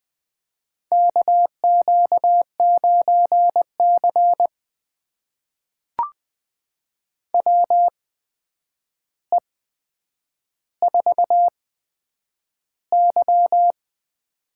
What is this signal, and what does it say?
Morse code, heard as a keyed tone, about 20 wpm, spaced as for 9 wpm, 700 Hz. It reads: KQ9C WE4Y
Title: USA Call Signs - 4 Characters - 8x Character Spacing - 20wpm